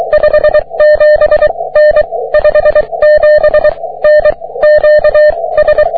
Morse code, heard as a keyed tone, about 23 wpm, 600 Hz. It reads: 57 N 57 N Q S
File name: wp / js6cqy 30m cw